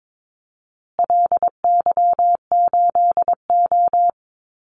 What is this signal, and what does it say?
Morse code, heard as a keyed tone, about 22 wpm, 700 Hz. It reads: LY8O
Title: Call Signs Repeated 2x - 22wpm